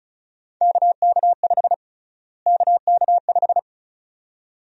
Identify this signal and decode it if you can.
Morse code, heard as a keyed tone, about 34 wpm, 700 Hz. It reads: KK5 KK5
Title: USA Call Sign Prefixes - Repeated 2x - 35wpm